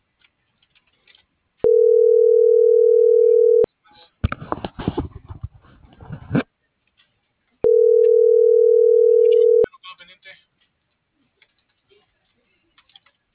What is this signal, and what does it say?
Ringback tone (North America), 440 + 480 Hz, with speech.
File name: CENTRAL SAGOT